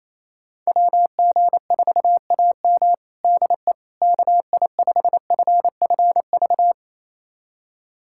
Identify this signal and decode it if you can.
Morse code, heard as a keyed tone, about 27 wpm, 700 Hz. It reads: WG4AM DE KI5FFV
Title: USA Callsign DE Callsign 28wpm